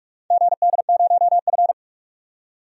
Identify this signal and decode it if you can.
Morse code, heard as a keyed tone, about 46 wpm, 700 Hz. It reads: GD0F
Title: DXCC-100 Call Signs - 4 Characters - 45wpm